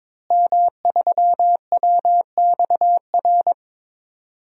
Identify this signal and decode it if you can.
Morse code, heard as a keyed tone, about 22 wpm, 700 Hz. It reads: M3WXR
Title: DXCC-100 Call Signs - 5 Characters - Repeated 3x - 22wpm